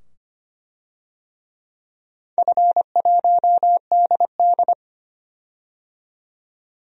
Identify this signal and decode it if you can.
Morse code, heard as a keyed tone, about 25 wpm, 700 Hz. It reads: F1DD
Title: DXCC-100 Call Signs - 4 Characters - 25wpm